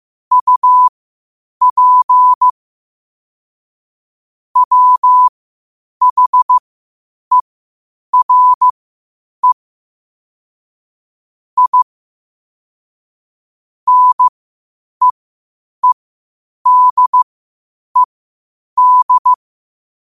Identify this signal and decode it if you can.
Morse code, heard as a keyed tone, about 15 wpm, spaced as for 8 wpm, 1000 Hz. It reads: UP WHERE I NEEDED
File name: qotdfiles / 8wpm